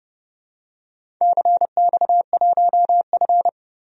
Morse code, heard as a keyed tone, about 28 wpm, 700 Hz. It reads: CX1F
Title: DXCC-100 Call Signs - 4 Characters - 30wpm